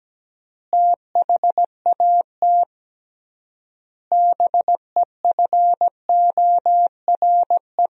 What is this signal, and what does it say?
Morse code, heard as a keyed tone, about 17 wpm, 700 Hz. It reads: THAT BEFORE
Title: 3x Word-Spacing - Sentences from Top 400 Words - 17wpm